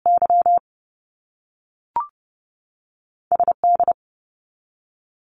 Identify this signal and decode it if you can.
Morse code, heard as a keyed tone, about 30 wpm, 700 Hz. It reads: Y SD
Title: US States 30wpm